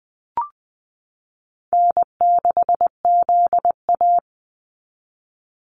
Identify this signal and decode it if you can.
Morse code, heard as a keyed tone, about 20 wpm, 700 Hz. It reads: N6ZA
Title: NA SOTA Activator Callsigns 20wpm